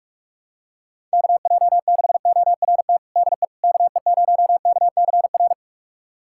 Morse code, heard as a keyed tone, about 41 wpm, 700 Hz. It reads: KJ6ORT DE KE0KCR
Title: USA Callsign DE Callsign - 6 Characters - 45wpm